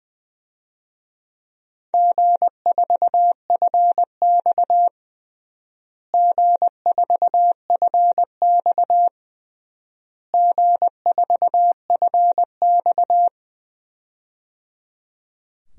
Morse code, heard as a keyed tone, about 20 wpm, 700 Hz. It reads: G4FX G4FX G4FX
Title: DXCC-100 Call Signs - 4 Characters - Repeated 3x - 20wpm